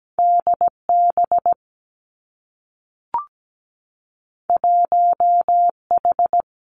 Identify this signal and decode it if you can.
Morse code, heard as a keyed tone, about 17 wpm, 700 Hz. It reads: DB 1H
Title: Two Challenging Letter-Number Combinations 17wpm